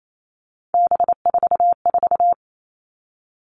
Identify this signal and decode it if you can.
Morse code, heard as a keyed tone, about 28 wpm, 700 Hz. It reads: B44
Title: All Cut Numbers - 3-Digit 28wpm